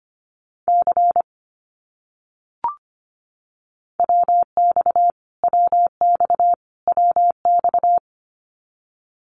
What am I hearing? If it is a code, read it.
Morse code, heard as a keyed tone, about 25 wpm, 700 Hz. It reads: C WX WX WX
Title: Abbreviations Repeated 3x - 25wpm